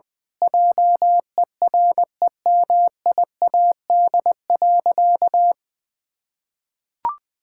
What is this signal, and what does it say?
Morse code, heard as a keyed tone, about 20 wpm, 700 Hz. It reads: JEREMIAD.
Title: Pangram Challenge v2 - 20wpm